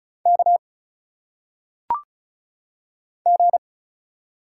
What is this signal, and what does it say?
Morse code, heard as a keyed tone, about 36 wpm, 700 Hz. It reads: K G
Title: Single Letters - Phonetic Pronunciation 35wpm